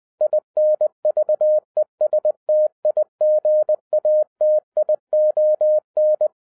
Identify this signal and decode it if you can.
Morse code, heard as a keyed tone, about 20 wpm, 610 Hz. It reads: INVESTIGATION